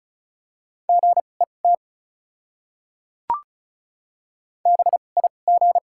Morse code, heard as a keyed tone, about 34 wpm, spaced as for 25 wpm, 700 Hz. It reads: GET BIG
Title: 2x Character Spacing - Top 200 Words - 35wpm